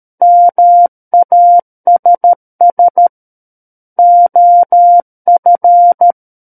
Morse code, heard as a keyed tone, about 13 wpm, 700 Hz. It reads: MASS OF
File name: Book One / 04-Chapter 4 - 13 WPM